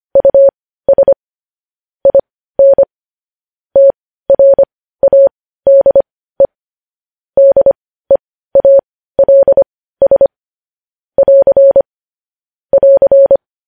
Morse code, heard as a keyed tone, about 25 wpm, spaced as for 15 wpm, 550 Hz. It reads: US IN TRADE DEALS <AR> <AR>
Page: News Headlines in Morse Code at 15 WPM